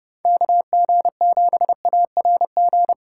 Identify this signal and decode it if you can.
Morse code, heard as a keyed tone, about 36 wpm, 700 Hz. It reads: KG7ARG
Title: Speed Racing - USA 2x3 Call Signs 30wpm